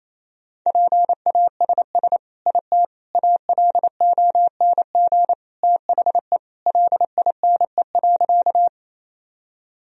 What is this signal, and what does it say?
Morse code, heard as a keyed tone, about 28 wpm, 700 Hz. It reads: PASS IT ALONG THE LINE.
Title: Sentences from Top 1000 Words - Review Part 1 - 28wpm